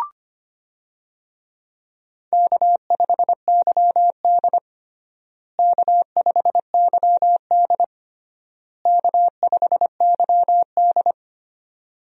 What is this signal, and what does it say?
Morse code, heard as a keyed tone, about 25 wpm, 700 Hz. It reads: K5YD K5YD K5YD
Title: USA Call Signs - 4 Characters - Repeated 3x - 25wpm